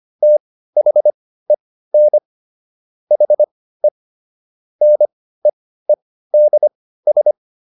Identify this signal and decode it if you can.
Morse code, heard as a keyed tone, about 25 wpm, spaced as for 15 wpm, 600 Hz. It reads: THEN HE NEEDS